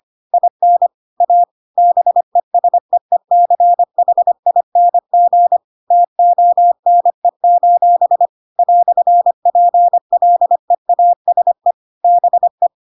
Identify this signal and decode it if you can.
Morse code, heard as a keyed tone, about 25 wpm, 700 Hz. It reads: IN A BESEECHING TONE: "PLEASE BE
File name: Pinocchio / Pinocchio ch1 25 25 ik1hge